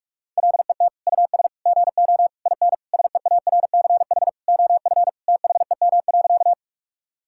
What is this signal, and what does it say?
Morse code, heard as a keyed tone, about 42 wpm, 700 Hz. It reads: LET US GO IN SEARCH OF THEM.
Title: Sentences from Top 1000 Words - Review Part 5 - 45wpm